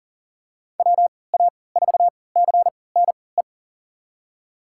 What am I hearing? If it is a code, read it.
Morse code, heard as a keyed tone, about 40 wpm, spaced as for 23 wpm, 700 Hz. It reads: WA4CNE
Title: USA Call Signs - 3x Character Spacing - 40wpm